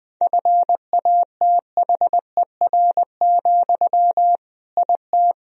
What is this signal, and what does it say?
Morse code, heard as a keyed tone, about 20 wpm, 700 Hz. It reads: FATHER, IT
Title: Sentences from Top 1000 Words - Review Part 2 - 20wpm